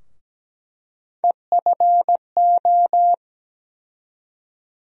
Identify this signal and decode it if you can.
Morse code, heard as a keyed tone, about 17 wpm, 700 Hz. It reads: EFO
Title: USA Call Sign Postfixes - 3 Letters 17wpm